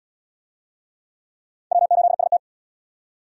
Morse code, heard as a keyed tone, about 66 wpm, 700 Hz. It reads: W8ST